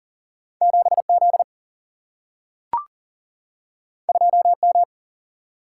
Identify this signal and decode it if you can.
Morse code, heard as a keyed tone, about 40 wpm, 700 Hz. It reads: 7 Z 2 M
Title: Single-Letter Single-Number 40wpm